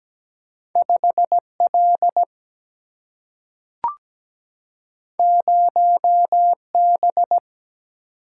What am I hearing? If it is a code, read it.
Morse code, heard as a keyed tone, about 17 wpm, 700 Hz. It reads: 5L 0B
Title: Single-Letter Single-Number 17wpm